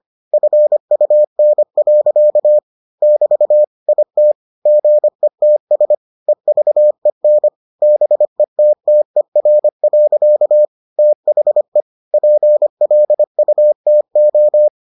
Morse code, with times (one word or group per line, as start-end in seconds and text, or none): 0.3-2.6 FUN.
3.0-3.6 <BT>
3.9-4.3 IT
4.7-5.9 GETS
6.3-7.5 EVEN
7.8-10.7 BETTER.
11.0-11.8 THE
12.1-14.7 PLUTO